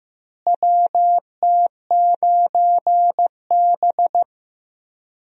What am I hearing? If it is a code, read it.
Morse code, heard as a keyed tone, about 15 wpm, 700 Hz. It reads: WT9B
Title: USA Call Signs - 4 Characters - Repeated 3x - 15wpm